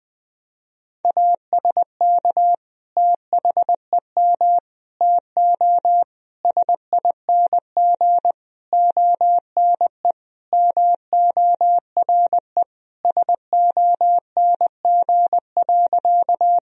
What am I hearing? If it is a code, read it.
Morse code, heard as a keyed tone, about 20 wpm, 700 Hz. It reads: ASK THEM TO SING ONE MORE SONG.
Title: Sentences from Top 1000 Words - Review Part 2 - 20wpm